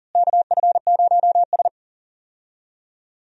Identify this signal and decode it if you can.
Morse code, heard as a keyed tone, about 40 wpm, 700 Hz. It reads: KF0S
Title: USA Call Signs - 4 Characters - Repeated 3x - 40wpm